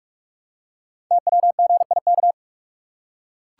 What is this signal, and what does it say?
Morse code, heard as a keyed tone, about 44 wpm, 700 Hz. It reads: TWGIK
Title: Code Groups - 5 Letters - 45wpm